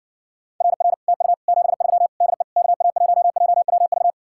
Morse code, heard as a keyed tone, about 60 wpm, 700 Hz. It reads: RR TU 73 DE KA1JWV